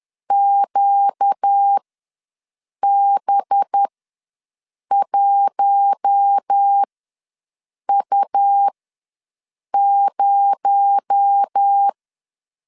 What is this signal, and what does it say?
Morse code, heard as a keyed tone, about 10 wpm, spaced as for 6 wpm, 800 Hz. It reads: QB1U0